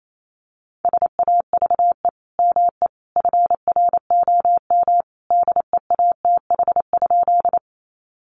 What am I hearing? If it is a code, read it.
Morse code, heard as a keyed tone, about 28 wpm, 700 Hz. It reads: SAVE ME FROM DEATH?
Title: Sentences from Top 1100 Words 28wpm